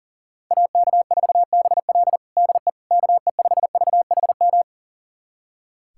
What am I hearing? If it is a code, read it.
Morse code, heard as a keyed tone, about 40 wpm, 700 Hz. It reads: AK4BL DE KE5VHM
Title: USA Callsign DE Callsign 40wpm